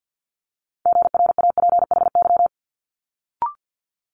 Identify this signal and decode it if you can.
Morse code, heard as a keyed tone, about 50 wpm, 700 Hz. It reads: GRAPHY
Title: Multi-letter Phonemes Quick Fire 50wpm